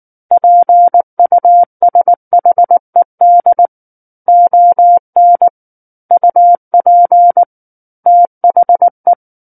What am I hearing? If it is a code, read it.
Morse code, heard as a keyed tone, about 19 wpm, 700 Hz. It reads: PUSHED ON UP THE
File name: Book One / 10-Chapter 10 - 19 WPM